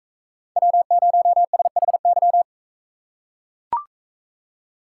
Morse code, heard as a keyed tone, about 42 wpm, 700 Hz. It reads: W0SHY